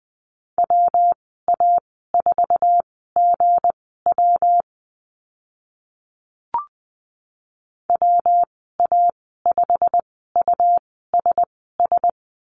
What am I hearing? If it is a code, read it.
Morse code, heard as a keyed tone, about 20 wpm, spaced as for 14 wpm, 700 Hz. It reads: WA4GW WA5USS
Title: 179 Lesson 29g - Slash - Call Signs 20wpm